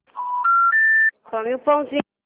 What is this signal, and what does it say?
Special information tone, with speech.
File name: out- -2003- - - .384